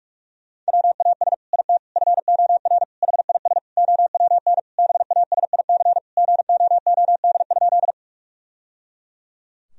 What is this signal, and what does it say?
Morse code, heard as a keyed tone, about 47 wpm, 700 Hz. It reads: WAS IT FOR HIS OWN BASIC GOOD?
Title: Sentences from Top 1000 Words - Review Part 3 - 45wpm